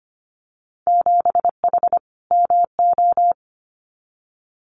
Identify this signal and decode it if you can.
Morse code, heard as a keyed tone, about 25 wpm, 700 Hz. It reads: 7H MO
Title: Winter Field Day Exchanges Encore 25wpm